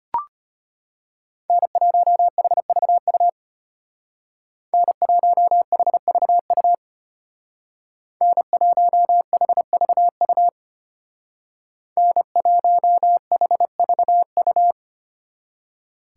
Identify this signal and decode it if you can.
Morse code, heard as a keyed tone, about 30 wpm, 700 Hz. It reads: N1HVU N1HVU N1HVU N1HVU
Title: Speed Racing - USA 1x3 Call Signs 25wpm